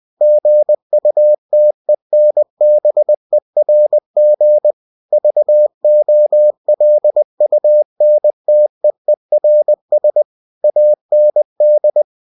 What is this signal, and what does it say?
Morse code, heard as a keyed tone, about 20 wpm, 600 Hz. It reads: GUTENBERG VOLUNTEERS AND